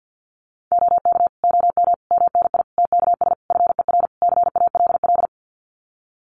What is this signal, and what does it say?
Morse code, heard as a keyed tone, about 53 wpm, 700 Hz. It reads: OK OM MNI TKS FER CALL
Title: Common QSO Phrases 50wpm